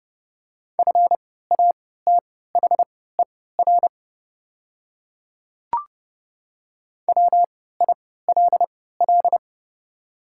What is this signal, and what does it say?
Morse code, heard as a keyed tone, about 31 wpm, spaced as for 17 wpm, 700 Hz. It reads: FATHER WILL
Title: 3x Character Spacing - Top 300 Words - 30wpm